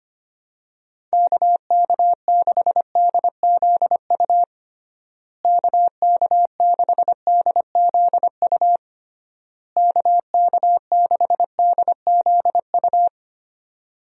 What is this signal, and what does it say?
Morse code, heard as a keyed tone, about 25 wpm, 700 Hz. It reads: KK6DZU KK6DZU KK6DZU
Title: USA Call Signs - 6 Characters - Repeated 3x - 25wpm